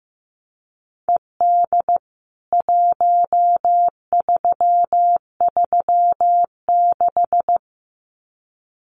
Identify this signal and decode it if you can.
Morse code, heard as a keyed tone, about 15 wpm, 700 Hz. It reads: ED 1336
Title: CWT Exchange 15wpm